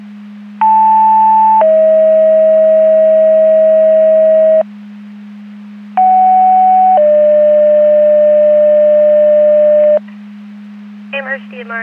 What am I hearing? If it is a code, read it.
Two-tone page, heard as a one-second tone, then a three-second tone, about 880 then 640 Hz, about 750 then 600 Hz. It is heard from 0.6 to 10.0 s.
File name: Portage 154.385 Paging